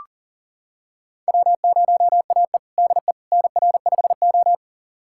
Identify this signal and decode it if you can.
Morse code, heard as a keyed tone, about 40 wpm, 700 Hz. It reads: W0AE DE NR5O
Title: USA Callsign DE Callsign - 4 Characters - 40wpm